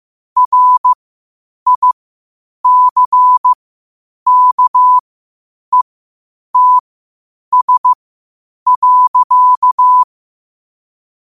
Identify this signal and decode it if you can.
Morse code, heard as a keyed tone, about 15 wpm, spaced as for 8 wpm, 1000 Hz. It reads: RICKETS.
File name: qotdfiles / 8wpm